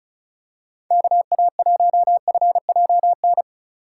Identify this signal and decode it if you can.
Morse code, heard as a keyed tone, about 36 wpm, 700 Hz. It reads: KA1FJN
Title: USA Call Signs - 6 Characters - 3x Character Spacing - 35wpm